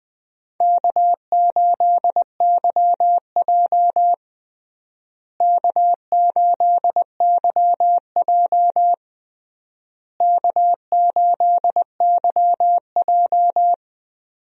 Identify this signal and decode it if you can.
Morse code, heard as a keyed tone, about 20 wpm, 700 Hz. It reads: K8YJ K8YJ K8YJ
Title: USA Call Signs - 4 Characters - Repeated 3x - 20wpm